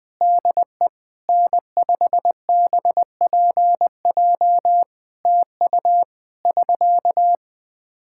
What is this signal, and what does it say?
Morse code, heard as a keyed tone, about 21 wpm, 700 Hz. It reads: DE N5BPJ TU <SK>
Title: Practice QSO - Exchange 4 - Ending - 20wpm